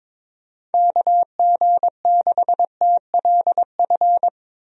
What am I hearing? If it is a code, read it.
Morse code, heard as a keyed tone, about 23 wpm, 700 Hz. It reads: KG6TLF
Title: USA Call Signs - 6 Characters - Repeated 3x - 22wpm